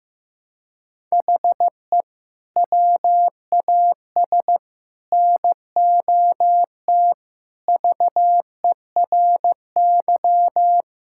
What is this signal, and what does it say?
Morse code, heard as a keyed tone, about 15 wpm, 700 Hz. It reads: HE WAS NOT VERY
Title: Sentences from Top 1000 Words - Review Part 5 - 15wpm